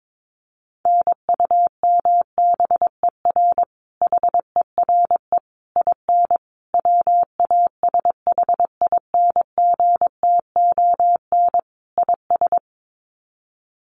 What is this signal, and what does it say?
Morse code, heard as a keyed tone, about 22 wpm, 700 Hz. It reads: NUMBER HERE IN WASHINGTON IS
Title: Sets of 5 Words Encore2 22wpm